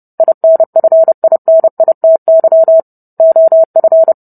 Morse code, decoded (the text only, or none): INFINITY OF